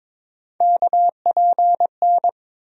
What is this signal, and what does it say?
Morse code, heard as a keyed tone, about 21 wpm, 700 Hz. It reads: KPN
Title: USA Call Sign Postfixes - 3 Letters 22wpm